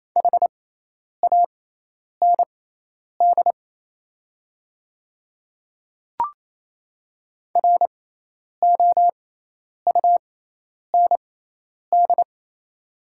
Morse code, heard as a keyed tone, about 28 wpm, spaced as for 10 wpm, 700 Hz. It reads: HAND ROUND
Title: 6x Character Spacing - Top 500 Words - 28wpm